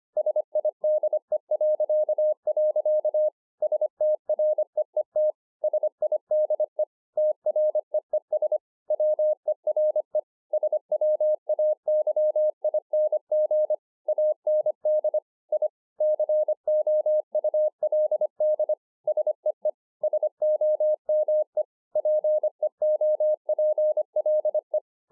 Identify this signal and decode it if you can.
Morse code, heard as a keyed tone, about 24 wpm, 610 Hz. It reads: SIDE.. STREET SIDE TREES WERE SWAYING AND I COULD SEE SOME PEOPLE